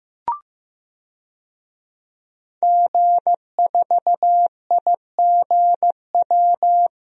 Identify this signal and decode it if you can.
Morse code, heard as a keyed tone, about 16 wpm, 700 Hz. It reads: G4IGW
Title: DXCC-100 Call Signs - 5 Characters - Repeated 2x - 15wpm